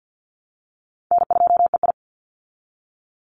Morse code, heard as a keyed tone, about 50 wpm, 700 Hz. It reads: N2EI
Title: Call Signs Repeated 2x - 50wpm